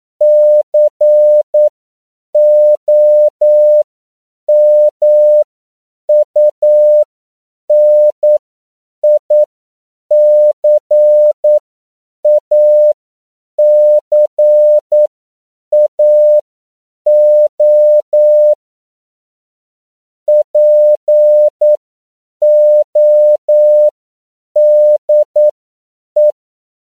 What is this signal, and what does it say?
Morse code, heard as a keyed tone, about 9 wpm, spaced as for 7 wpm, 600 Hz. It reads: COMUNICACAO PODE
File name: CW / 10PPM